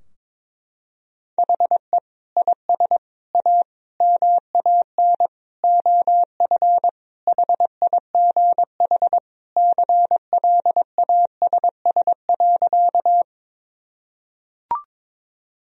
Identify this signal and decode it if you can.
Morse code, heard as a keyed tone, about 22 wpm, 700 Hz. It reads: HE IS A MAN OF HIGH CLASS.
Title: Sentences from Top 1000 Words - Review Part 2 - 22wpm